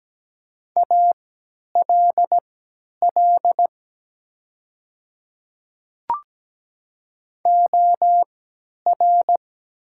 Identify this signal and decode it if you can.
Morse code, heard as a keyed tone, about 17 wpm, spaced as for 10 wpm, 700 Hz. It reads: ALL OR
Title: 3x Character Spacing - Top 100 Words - 17wpm